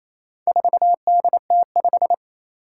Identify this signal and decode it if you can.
Morse code, heard as a keyed tone, about 27 wpm, 700 Hz. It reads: 4DT5
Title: Four Challenging Letter-Number Combinations 28wpm